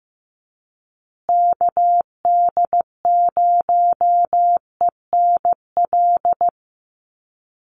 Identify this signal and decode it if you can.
Morse code, heard as a keyed tone, about 15 wpm, 700 Hz. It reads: KD0ENL
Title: USA Call Signs Repeated 3x 15wpm